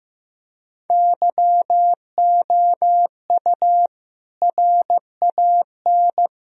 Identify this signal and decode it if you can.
Morse code, heard as a keyed tone, about 15 wpm, 700 Hz. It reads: YOU RAN
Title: Sentences from Top 1000 Words - Review Part 3 - 15wpm